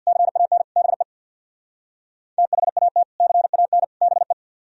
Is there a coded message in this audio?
Yes: Morse code from 0.1 to 4.3 s, heard as a keyed tone, about 65 wpm, 700 Hz. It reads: CAN BE THAT CAN BE